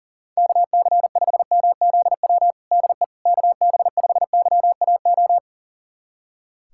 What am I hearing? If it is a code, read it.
Morse code, heard as a keyed tone, about 40 wpm, 700 Hz. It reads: KC5MZW DE KB5YAO
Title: USA Callsign DE Callsign - 6 Characters - 40wpm